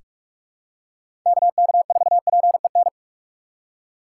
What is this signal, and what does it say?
Morse code, heard as a keyed tone, about 46 wpm, 690 Hz. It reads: KK4PEN
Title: USA Call Signs - 3x Character Spacing - 45wpm